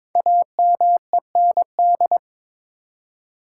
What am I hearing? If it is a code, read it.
Morse code, heard as a keyed tone, about 22 wpm, 700 Hz. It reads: AMEND
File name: 5 Letter Words - First 3 Letters Given - 22wpm